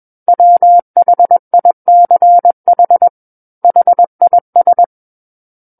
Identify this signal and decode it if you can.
Morse code, heard as a keyed tone, about 21 wpm, 700 Hz. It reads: WHICH HIS
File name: Book One / 12-Chapter 12 - 21 WPM